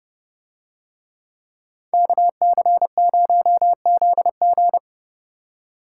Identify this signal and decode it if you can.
Morse code, heard as a keyed tone, about 29 wpm, 700 Hz. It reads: KC0ZG